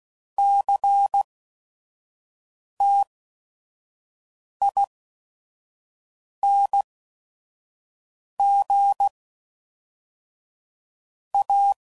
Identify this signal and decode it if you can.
Morse code, heard as a keyed tone, about 16 wpm, spaced as for 5 wpm, 800 Hz. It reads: CTING A
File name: CW Code Groups @5WPM / QSOs / QSO WA7QOX 1